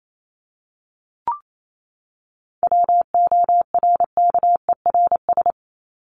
Morse code, heard as a keyed tone, about 28 wpm, 700 Hz. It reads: WORKERS
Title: ES:S Suffix - Top 100 Words with Base Words 28wpm